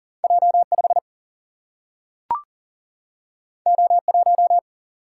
Morse code, heard as a keyed tone, about 40 wpm, 700 Hz. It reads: J5 O1
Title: Single-Letter Single-Number 40wpm